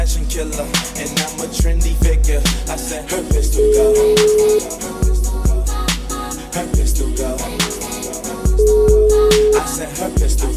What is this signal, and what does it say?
Ringback tone (Europe, 425 Hz).